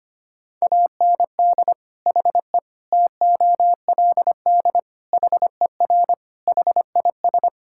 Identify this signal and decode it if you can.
Morse code, heard as a keyed tone, about 25 wpm, 700 Hz. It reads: AND HE TOLD HER HIS
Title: Sentences from Top 1000 Words - Review Part 5 - 25wpm